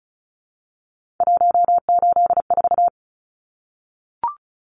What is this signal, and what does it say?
Morse code, heard as a keyed tone, about 34 wpm, 700 Hz. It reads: 184
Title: Numbers - 3-Digit 35wpm